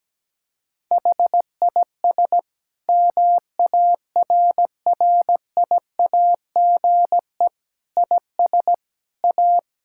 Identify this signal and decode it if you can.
Morse code, heard as a keyed tone, about 17 wpm, 700 Hz. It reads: HIS MARRIAGE IS A
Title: Sentences from Top 1000 Words - Review Part 6 - 17wpm